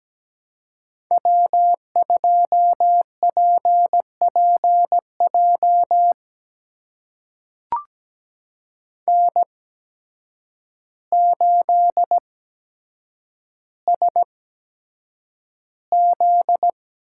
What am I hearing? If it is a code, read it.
Morse code, heard as a keyed tone, about 17 wpm, spaced as for 8 wpm, 700 Hz. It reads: W2PPJ N8SZ
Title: USA Call Signs - 5 Characters - 8x Character Spacing - 17wpm